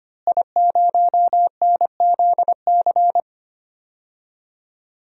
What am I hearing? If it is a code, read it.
Morse code, heard as a keyed tone, about 25 wpm, 700 Hz. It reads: I0NZC
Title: DXCC-100 Call Signs - 5 Characters - Repeated 3x - 25wpm